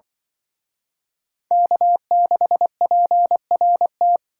Morse code, heard as a keyed tone, about 25 wpm, 700 Hz. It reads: K6PRT
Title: Speed Racing - USA 1x3 Call Signs 20wpm